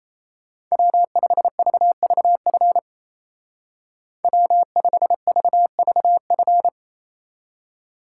Morse code, heard as a keyed tone, about 33 wpm, 700 Hz. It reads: W5VVF W5VVF
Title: Speed Racing - USA 1x3 Call Signs 28wpm